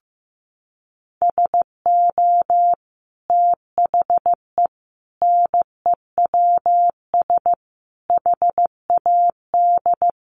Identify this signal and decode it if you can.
Morse code, heard as a keyed tone, about 15 wpm, 700 Hz. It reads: SO THE NEWS HAD
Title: Sentences from Top 1200 Words 15wpm